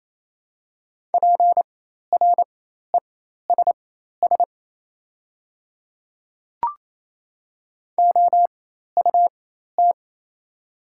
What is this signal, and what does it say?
Morse code, heard as a keyed tone, about 28 wpm, spaced as for 13 wpm, 700 Hz. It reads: PRESS OUT
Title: 4x Character Spacing - Top 300 Words - 28wpm